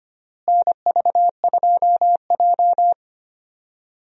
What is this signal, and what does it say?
Morse code, heard as a keyed tone, about 25 wpm, 700 Hz. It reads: NV2J
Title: USA Call Signs - 4 Characters - Repeated 3x - 25wpm